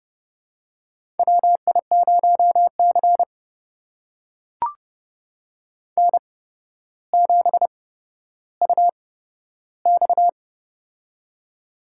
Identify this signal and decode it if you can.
Morse code, heard as a keyed tone, about 28 wpm, spaced as for 13 wpm, 700 Hz. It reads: WI0C N7UX
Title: USA Call Signs - 4 Characters - 8x Character Spacing - 30wpm